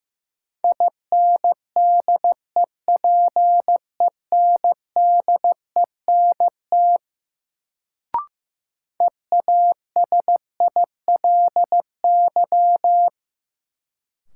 Morse code, heard as a keyed tone, about 16 wpm, 700 Hz. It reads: INDEPENDENT EASILY
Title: Sentences from Top 900 Words - New Word Drill - 15wpm